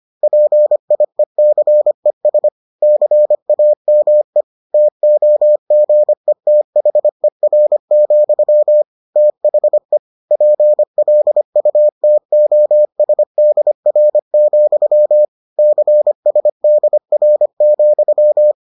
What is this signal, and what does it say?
Morse code, heard as a keyed tone, about 25 wpm, 600 Hz. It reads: PIECES CAME TOGETHER, THE PLUTOSDR, CSDR,